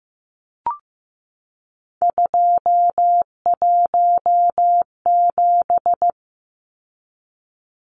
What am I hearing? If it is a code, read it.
Morse code, heard as a keyed tone, about 15 wpm, 700 Hz. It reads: 217
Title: Numbers - 3-Digit 15wpm